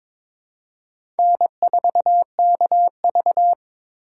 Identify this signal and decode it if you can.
Morse code, heard as a keyed tone, about 22 wpm, 700 Hz. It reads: N4KV
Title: USA Call Signs - 4 Characters - 3x Character Spacing - 22wpm